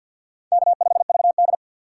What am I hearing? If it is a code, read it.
Morse code, heard as a keyed tone, about 50 wpm, 700 Hz. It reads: K5VD
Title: USA Call Signs - 4 Characters - 6x Character Spacing - 50wpm